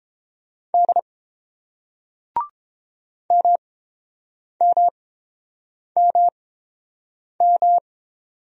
Morse code, heard as a keyed tone, about 30 wpm, spaced as for 7 wpm, 700 Hz. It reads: D MMMM